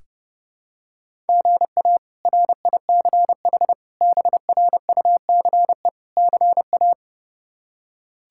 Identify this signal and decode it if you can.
Morse code, heard as a keyed tone, about 32 wpm, 700 Hz. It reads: GA RICH BRUCE CA
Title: K1USN SST QSO Practice - 30wpm